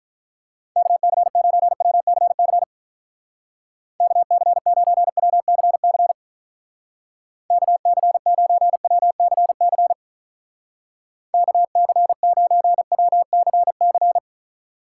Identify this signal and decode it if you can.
Morse code, heard as a keyed tone, about 44 wpm, 700 Hz. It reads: KC9WCC KC9WCC KC9WCC KC9WCC
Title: Speed Racing - USA 2x3 Call Signs 35wpm